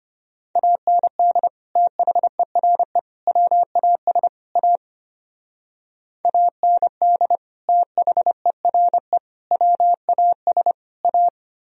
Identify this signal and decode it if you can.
Morse code, heard as a keyed tone, about 30 wpm, 700 Hz. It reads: AND THERE WAS A AND THERE WAS A
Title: Speed-Racing - Sets of 4 Words 25wpm